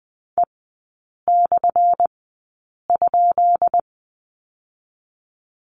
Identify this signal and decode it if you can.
Morse code, heard as a keyed tone, about 20 wpm, spaced as for 8 wpm, 700 Hz. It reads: E/?
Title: 198 Lesson 32e - Letter k - Three Characters 20wpm